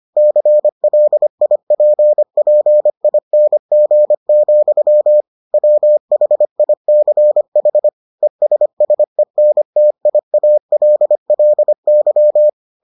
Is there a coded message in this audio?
Yes: Morse code from 0.2 to 12.5 s, heard as a keyed tone, about 25 wpm, 600 Hz. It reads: CLIPPING, WHICH ESSENTIALLY